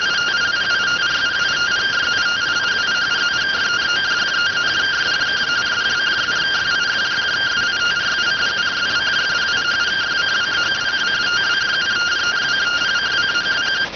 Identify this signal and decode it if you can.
Two-tone teleprinter signal, about 170 Hz shift.